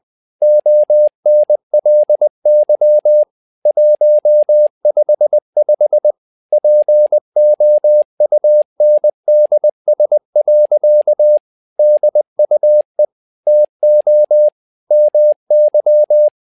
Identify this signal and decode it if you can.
Morse code, heard as a keyed tone, about 20 wpm, 600 Hz. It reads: ONLY 155 POUNDS. DUE TO MY
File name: learning center / tower trouble / tower trouble 20wpm